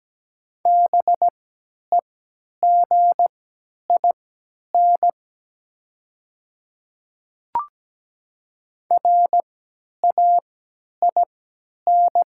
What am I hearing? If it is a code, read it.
Morse code, heard as a keyed tone, about 17 wpm, spaced as for 10 wpm, 700 Hz. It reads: BEGIN RAIN
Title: 3x Character Spacing - Top 300 Words - 17wpm